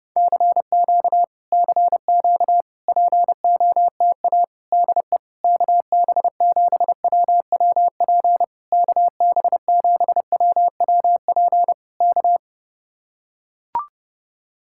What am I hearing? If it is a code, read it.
Morse code, heard as a keyed tone, about 31 wpm, 700 Hz. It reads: CQ CQ POTA DE KB7WWP KB7WWP K